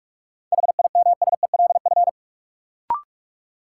Morse code, heard as a keyed tone, about 42 wpm, 700 Hz. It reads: HIMSELF
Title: Sentences from Top 1500 Words - New Word Drill - 45wpm